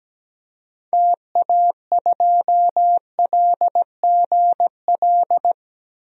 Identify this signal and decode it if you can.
Morse code, heard as a keyed tone, about 17 wpm, 700 Hz. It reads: TA2LGL
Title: DXCC-100 Call Signs - 6 Characters - 17wpm